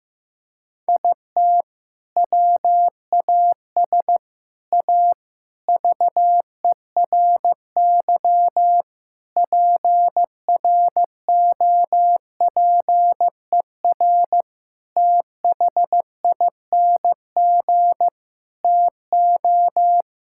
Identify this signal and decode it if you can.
Morse code, heard as a keyed tone, about 15 wpm, 700 Hz. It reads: IT WAS A VERY PROPER THING TO
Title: Sentences from Top 1000 Words - Review Part 5 - 15wpm